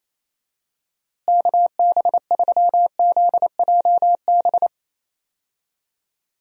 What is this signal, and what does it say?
Morse code, heard as a keyed tone, about 28 wpm, 700 Hz. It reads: KB3ZJB
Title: USA Call Signs - 6 Characters - Repeated 3x - 28wpm